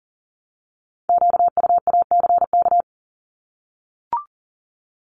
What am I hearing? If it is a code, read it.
Morse code, heard as a keyed tone, about 40 wpm, 700 Hz. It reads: QUACK
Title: Multi-letter Phonemes Repeated 3x 40wpm